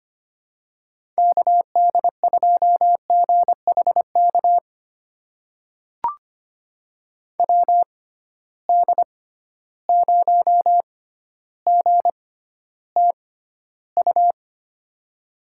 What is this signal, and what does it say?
Morse code, heard as a keyed tone, about 25 wpm, spaced as for 13 wpm, 700 Hz. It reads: KD2GHK WD0GTU
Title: USA Call Signs - 6 Characters - 6x Character Spacing - 25wpm